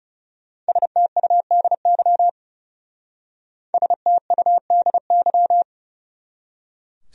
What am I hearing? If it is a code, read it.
Morse code, heard as a keyed tone, about 37 wpm, 700 Hz. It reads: STUDY STUDY
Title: Speed-Racing Top 300 Words 30wpm